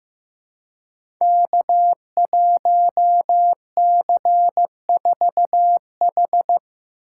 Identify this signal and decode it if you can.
Morse code, heard as a keyed tone, about 15 wpm, 700 Hz. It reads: K1C4H
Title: Code Groups - 5 Letter:Numbers - 15wpm